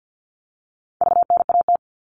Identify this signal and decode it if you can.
Morse code, heard as a keyed tone, about 50 wpm, 700 Hz. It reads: VNAT
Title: All Cut Numbers - 4-Digit 50wpm